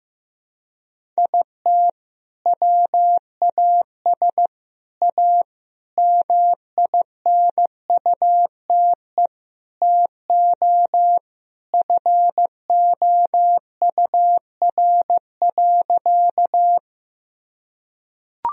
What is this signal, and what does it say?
Morse code, heard as a keyed tone, about 15 wpm, 700 Hz. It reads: IT WAS A MINUTE TO FOUR.
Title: Sentences from Top 1000 Words - Review Part 1 - 15wpm